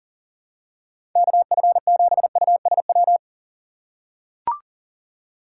Morse code, heard as a keyed tone, about 40 wpm, 700 Hz. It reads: KF7USW
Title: USA Call Signs - 6 Characters - 4x Character Spacing - 40wpm